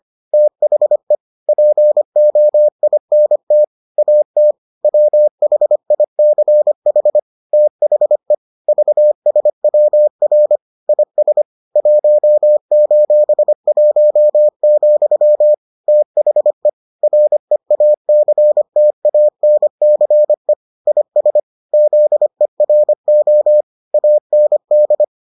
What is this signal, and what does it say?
Morse code, heard as a keyed tone, about 25 wpm, 600 Hz. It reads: THE POINT AT WHICH THE VSWR IS 1:1, THE REACTANCE IS ZERO AND